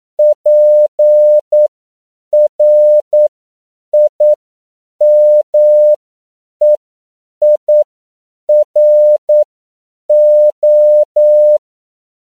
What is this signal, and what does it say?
Morse code, heard as a keyed tone, about 9 wpm, spaced as for 7 wpm, 600 Hz. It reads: PRIMEIRO
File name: CW / 10PPM